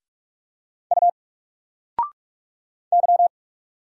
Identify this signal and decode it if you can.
Morse code, heard as a keyed tone, about 44 wpm, 700 Hz. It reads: U Y
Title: Single Character (Letter:Number:Punctuation:Prosign) 45wpm